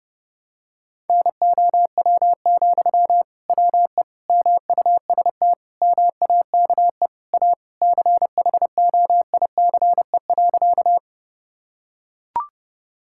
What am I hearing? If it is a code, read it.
Morse code, heard as a keyed tone, about 30 wpm, 700 Hz. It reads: NOW, WE MUST MAKE A CHOICE.
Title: Sentences from Top 1000 Words - Review Part 6 - 30wpm